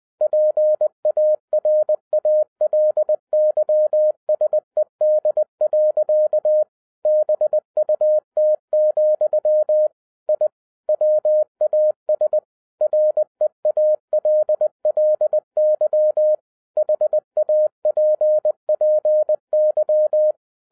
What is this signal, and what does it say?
Morse code, heard as a keyed tone, about 20 wpm, 610 Hz. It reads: PARALYSED. BUT, I WAS REALLY HAPPY
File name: 2011-07-mp3 / n58 BACK FROM A BIG ACCIDENT 20wpm